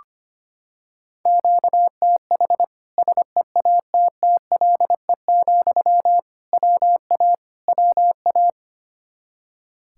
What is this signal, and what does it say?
Morse code, heard as a keyed tone, about 25 wpm, 700 Hz. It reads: QTH SEATTLE, WA WA
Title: Practice QSO - Exchange 1 - Beginning - 25wpm